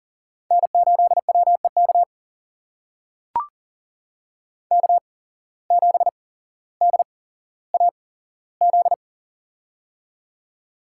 Morse code, heard as a keyed tone, about 40 wpm, spaced as for 17 wpm, 700 Hz. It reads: N8WEK K7DAZ
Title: USA Call Signs - 5 Characters - 8x Character Spacing - 40wpm